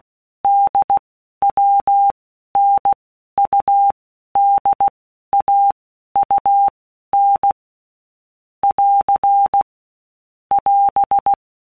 Morse code, heard as a keyed tone, about 16 wpm, 800 Hz. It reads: D W N U D A U N <AR> <AS>
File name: All Groups 1-minute Blocks / Groups@15 / AWUNDG